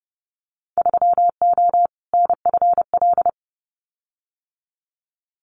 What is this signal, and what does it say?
Morse code, heard as a keyed tone, about 30 wpm, 700 Hz. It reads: 3O NFL
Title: Winter Field Day Exchanges 30wpm